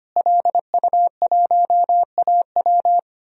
Morse code, heard as a keyed tone, about 25 wpm, 700 Hz. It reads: LU1AW
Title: DXCC-100 Call Signs - 5 Characters - Repeated 2x - 25wpm